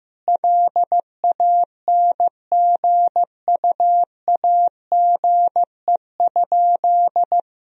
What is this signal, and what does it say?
Morse code, heard as a keyed tone, about 16 wpm, 700 Hz. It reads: LANGUAGE?
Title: Sentences from Top 1000 Words - Review Part 3 - 15wpm